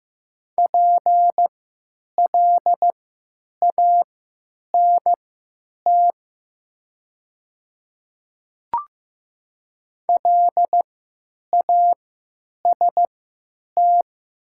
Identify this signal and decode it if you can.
Morse code, heard as a keyed tone, about 16 wpm, spaced as for 9 wpm, 700 Hz. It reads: PLANT LAST
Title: 3x Character Spacing - Top 500 Words - 15wpm